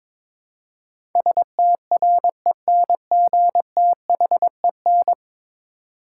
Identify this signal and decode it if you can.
Morse code, heard as a keyed tone, about 22 wpm, 700 Hz. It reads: STRENGTHEN
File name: practice / 22wpm / 10 Letter Words - First 3 Letters Given - 22wpm